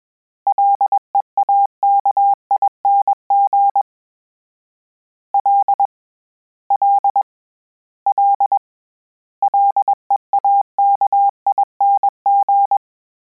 Morse code, heard as a keyed tone, about 21 wpm, 800 Hz. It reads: LEAKING L L L LEAKING
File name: Combinations / Combinations-mp3s / Combinations@20